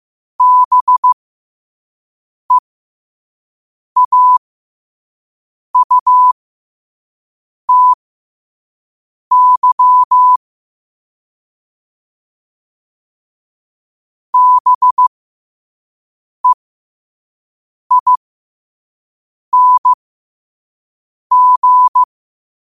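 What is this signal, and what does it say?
Morse code, heard as a keyed tone, about 15 wpm, spaced as for 5 wpm, 1000 Hz. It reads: BEAUTY BEING
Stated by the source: Morse code sent at 5 wpm.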